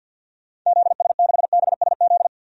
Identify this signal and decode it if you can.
Morse code, heard as a keyed tone, about 51 wpm, 700 Hz. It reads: ZS6BSZ
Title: DXCC-100 Call Signs - 6 Characters - 50wpm